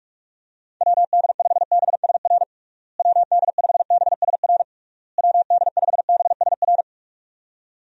Morse code, heard as a keyed tone, about 46 wpm, 700 Hz. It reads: WD5BSR WD5BSR WD5BSR
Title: USA Call Signs - 6 Characters - Repeated 3x - 45wpm